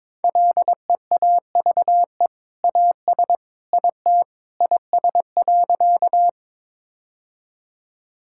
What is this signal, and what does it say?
Morse code, heard as a keyed tone, about 22 wpm, 700 Hz. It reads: LEAVE AS IT IS.
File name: practice / 22wpm / Sentences with 2-Word Phrases - 22wpm